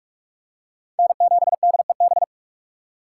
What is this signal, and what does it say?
Morse code, heard as a keyed tone, about 44 wpm, 700 Hz. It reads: N7DEB
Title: USA Call Signs - 5 Characters - 4x Character Spacing - 45wpm